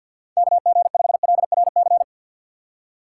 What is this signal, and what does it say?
Morse code, heard as a keyed tone, about 52 wpm, 700 Hz. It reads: KG5LRC
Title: USA Call Signs - 6 Characters - 6x Character Spacing - 50wpm